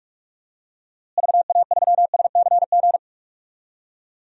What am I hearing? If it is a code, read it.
Morse code, heard as a keyed tone, about 47 wpm, 700 Hz. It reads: VA3SCG
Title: DXCC-100 Call Signs - 6 Characters - Repeated 2x - 45wpm